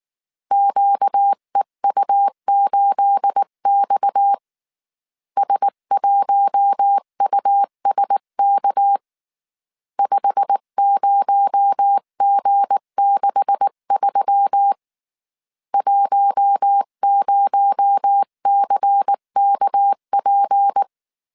Morse code, heard as a keyed tone, about 19 wpm, 800 Hz. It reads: QEU8X S1USK 50G63 10CKP
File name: CW / DET / 1 9 1500C S B 18wpm 2551